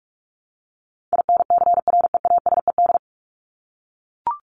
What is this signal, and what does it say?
Morse code, heard as a keyed tone, about 44 wpm, 700 Hz. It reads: INCREASED
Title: ED Suffix - Top 100 Words with Base Words 45wpm